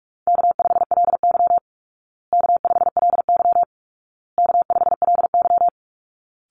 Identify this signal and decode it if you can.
Morse code, heard as a keyed tone, about 44 wpm, 700 Hz. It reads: K5LY K5LY K5LY
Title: Easy Call Signs Repeated 3x - 45wpm